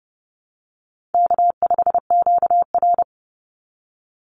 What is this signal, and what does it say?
Morse code, heard as a keyed tone, about 30 wpm, 700 Hz. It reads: K5QR
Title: NA SOTA Chaser Callsigns 30wpm